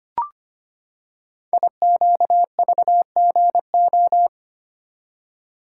Morse code, heard as a keyed tone, about 25 wpm, 700 Hz. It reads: IQVGO
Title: Code Groups - 5 Letters - 25wpm